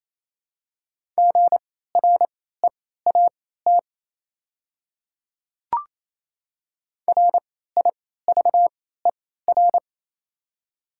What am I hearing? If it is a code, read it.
Morse code, heard as a keyed tone, about 28 wpm, spaced as for 16 wpm, 700 Hz. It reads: GREAT RIVER
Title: 3x Character Spacing - Top 300 Words - 28wpm